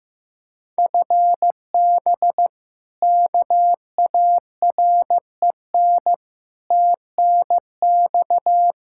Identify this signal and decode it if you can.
Morse code, heard as a keyed tone, about 15 wpm, 700 Hz. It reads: FB KAREN TNX
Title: K1USN SST QSO Practice 15wpm